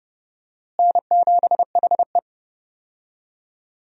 Morse code, heard as a keyed tone, about 30 wpm, 700 Hz. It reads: N7HE
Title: USA Call Signs - 4 Characters - Repeated 3x - 30wpm